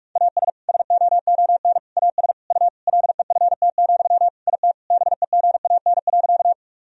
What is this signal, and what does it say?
Morse code, heard as a keyed tone, about 47 wpm, 700 Hz. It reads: AS SOON AS U LEFT, IT BEGAN.
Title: Sentences from Top 400 Words and Abbreviations 45wpm